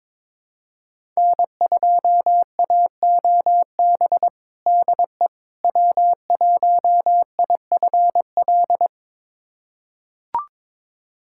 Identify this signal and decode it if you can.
Morse code, heard as a keyed tone, about 21 wpm, 700 Hz. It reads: N2AOB DE W1IFL
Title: USA Callsign DE Callsign - 5 Characters - 22wpm